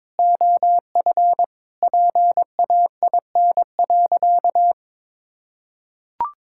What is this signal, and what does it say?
Morse code, heard as a keyed tone, about 21 wpm, 700 Hz. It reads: OF PAIN.
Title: Sentences from Top 1600 Words 22wpm